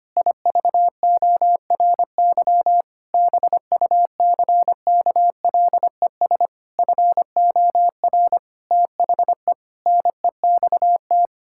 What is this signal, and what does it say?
Morse code, heard as a keyed tone, about 25 wpm, 700 Hz. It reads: IVORY BUCKLES FOR THE NEXT
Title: Pangram Challenge v2 - 25wpm